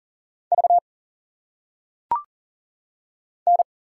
Morse code, heard as a keyed tone, about 38 wpm, 700 Hz. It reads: V N
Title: Single Letters - Phonetic Pronunciation 40wpm